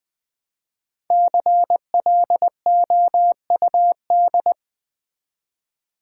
Morse code, heard as a keyed tone, about 20 wpm, 700 Hz. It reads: CLOUD
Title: Easy Words with Hints - 20wpm